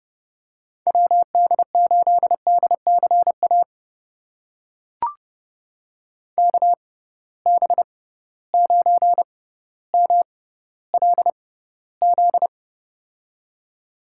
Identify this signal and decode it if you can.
Morse code, heard as a keyed tone, about 30 wpm, spaced as for 15 wpm, 700 Hz. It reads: WD8DCA KB9MLZ
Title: USA Call Signs - 6 Characters - 6x Character Spacing - 30wpm